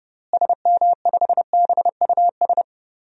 Morse code, heard as a keyed tone, about 32 wpm, 700 Hz. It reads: SM5BUS
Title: DXCC-100 Call Signs - 6 Characters - Repeated 2x - 30wpm